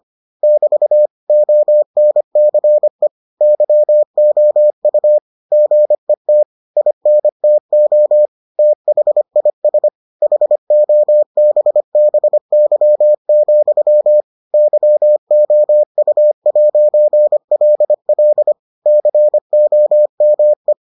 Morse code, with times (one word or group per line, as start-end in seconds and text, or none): 0.4-1.1 <BT>
1.3-3.1 ONCE
3.4-5.2 YOU
5.5-6.4 GET
6.8-8.3 INTO
8.6-9.9 THIS
10.2-14.2 HOBBY,
14.5-18.5 YOU'LL
18.9-20.7 COME